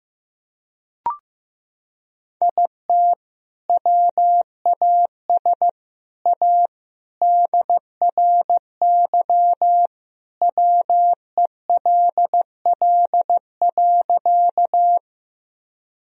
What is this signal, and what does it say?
Morse code, heard as a keyed tone, about 15 wpm, 700 Hz. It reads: IT WAS A DRY WELL.
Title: Sentences from Top 1000 Words - Review Part 3 - 15wpm